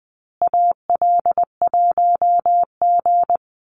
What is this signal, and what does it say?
Morse code, heard as a keyed tone, about 20 wpm, 700 Hz. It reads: AL1G
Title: Call Signs Repeated 2x - 20wpm